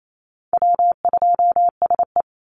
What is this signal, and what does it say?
Morse code, heard as a keyed tone, about 28 wpm, 700 Hz. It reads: W2SE
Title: NA SOTA Chaser Callsigns 28wpm